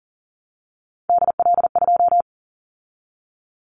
Morse code, heard as a keyed tone, about 41 wpm, 700 Hz. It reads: DL2
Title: Callsign Prefixes for Top 25 DXCC - 40wpm